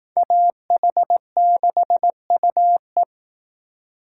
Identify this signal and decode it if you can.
Morse code, heard as a keyed tone, about 18 wpm, 700 Hz. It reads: AH6UE
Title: Speed Racing - USA 2x2 Call Signs 15wpm